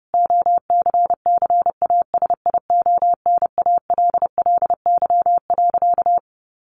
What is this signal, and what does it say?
Morse code, heard as a keyed tone, about 30 wpm, 700 Hz. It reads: OCCASIONALLY.
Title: Sentences from Top 1300 Words 30wpm